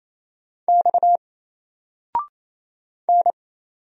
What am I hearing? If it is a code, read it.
Morse code, heard as a keyed tone, about 27 wpm, 700 Hz. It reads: X N
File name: Single-Letter-Number-28wpm